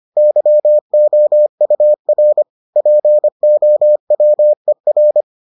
Morse code, heard as a keyed tone, about 25 wpm, 600 Hz. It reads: YOUR POWER